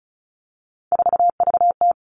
Morse code, heard as a keyed tone, about 34 wpm, 700 Hz. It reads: 4VT
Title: All Cut Numbers - 3-Digit 35wpm